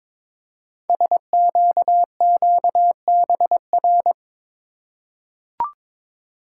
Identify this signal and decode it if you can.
Morse code, heard as a keyed tone, about 22 wpm, 700 Hz. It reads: SQQBR
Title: Code Groups - 5 Letters - 22wpm